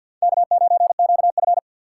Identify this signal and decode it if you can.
Morse code, heard as a keyed tone, about 46 wpm, 690 Hz. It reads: K9QF
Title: USA Call Signs - 4 Characters - 6x Character Spacing - 50wpm